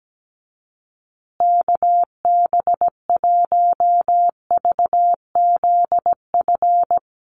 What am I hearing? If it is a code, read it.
Morse code, heard as a keyed tone, about 17 wpm, 700 Hz. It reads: KB1VZF
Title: USA Call Signs Repeated 2x 17wpm